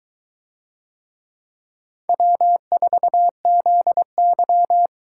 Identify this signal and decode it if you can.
Morse code, heard as a keyed tone, about 22 wpm, 700 Hz. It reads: W4ZY